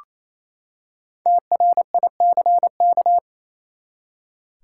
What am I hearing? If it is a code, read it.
Morse code, heard as a keyed tone, about 28 wpm, 700 Hz. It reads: TRICK
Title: Five Letter Words - 28wpm